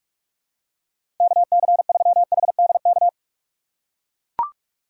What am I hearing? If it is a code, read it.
Morse code, heard as a keyed tone, about 44 wpm, 700 Hz. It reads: KC3HDK
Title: USA Call Signs - 6 Characters - 6x Character Spacing - 45wpm